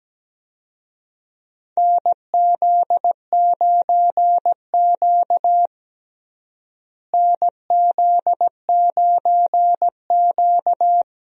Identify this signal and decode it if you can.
Morse code, heard as a keyed tone, about 17 wpm, 700 Hz. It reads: NZ9Q NZ9Q
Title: USA Call Signs - 4 Characters - Repeated 3x - 17wpm